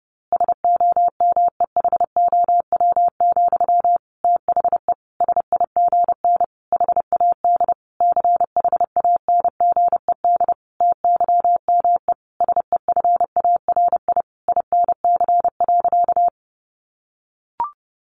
Morse code, heard as a keyed tone, about 28 wpm, 700 Hz. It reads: SOMEHOW, THE SIGN HAD CHANGED TYME SEFARI INC.
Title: A Sound Of Thunder 30wpm